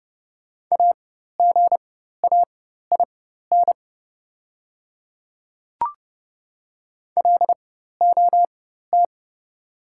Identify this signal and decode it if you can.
Morse code, heard as a keyed tone, about 29 wpm, spaced as for 14 wpm, 690 Hz. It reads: AGAIN LOT
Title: 4x Character Spacing - Top 500 Words - 30wpm